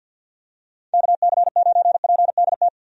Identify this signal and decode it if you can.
Morse code, heard as a keyed tone, about 49 wpm, 700 Hz. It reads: KC9PDT
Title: USA Call Signs - 6 Characters - 50wpm